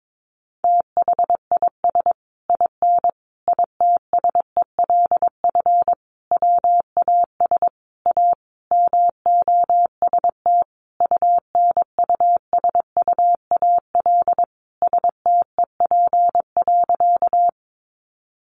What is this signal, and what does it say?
Morse code, heard as a keyed tone, about 22 wpm, 700 Hz. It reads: THIS IN ITSELF WAS A MOST UNUSUAL STEP.
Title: Sentences from Top 1400 Words 22wpm